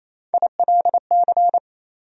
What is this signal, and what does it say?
Morse code, heard as a keyed tone, about 27 wpm, 700 Hz. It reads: ILC
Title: USA Call Sign Postfixes - 3 Letters 28wpm